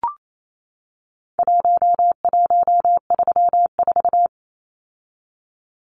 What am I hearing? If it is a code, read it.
Morse code, heard as a keyed tone, about 28 wpm, 700 Hz. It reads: 1134
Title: Numbers - Contest Style - 4-Digit 28wpm